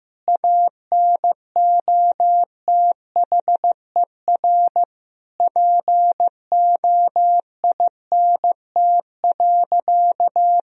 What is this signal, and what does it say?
Morse code, heard as a keyed tone, about 16 wpm, 700 Hz. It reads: ANOTHER POINT.
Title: Sentences from Top 1000 Words - Review Part 4 - 15wpm